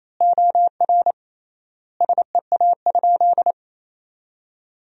Morse code, heard as a keyed tone, about 28 wpm, 700 Hz. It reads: OR SEA?
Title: 3x Word-Spacing - Sentences from Top 500 Words - 28wpm